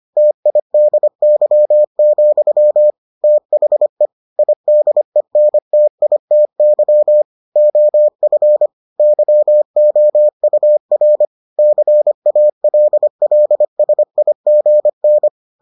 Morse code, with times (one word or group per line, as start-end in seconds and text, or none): 0.2-2.9 TIDY,
3.2-4.1 THE
4.4-7.2 IDENTITY
7.6-8.7 OF
9.0-11.2 YOUR
11.6-15.3 CALLSIGN